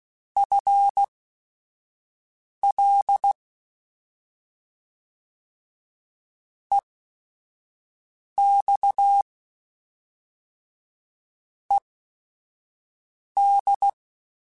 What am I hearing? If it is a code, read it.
Morse code, heard as a keyed tone, about 16 wpm, spaced as for 5 wpm, 800 Hz. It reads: FL EX ED